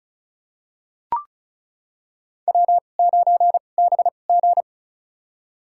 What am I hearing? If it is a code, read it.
Morse code, heard as a keyed tone, about 34 wpm, spaced as for 25 wpm, 700 Hz. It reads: W9BG
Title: USA Call Signs - 4 Characters - 2x Character Spacing - 35wpm